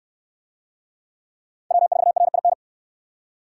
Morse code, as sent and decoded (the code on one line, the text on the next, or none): .-- ....- .-. .. -.
W4RIN